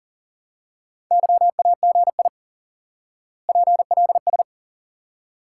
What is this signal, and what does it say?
Morse code, heard as a keyed tone, about 37 wpm, 700 Hz. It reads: YAGI PLS
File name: IQE Course 3 - Lesson 89 - YAGI - Two Elements - 40wpm